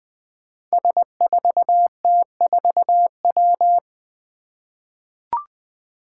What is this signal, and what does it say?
Morse code, heard as a keyed tone, about 20 wpm, 700 Hz. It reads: S4T4W
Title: Five Challenging Letter-Number Combinations 20wpm